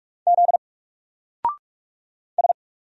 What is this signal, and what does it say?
Morse code, heard as a keyed tone, about 44 wpm, 700 Hz. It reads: Z S
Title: Single Character (Letter:Number:Punctuation:Prosign) Alt 45wpm